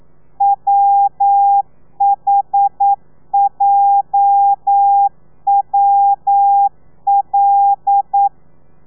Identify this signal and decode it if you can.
Morse code, heard as a keyed tone, about 9 wpm, 800 Hz. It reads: WHJWL
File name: telegrafia / i2tfj / Lez8 40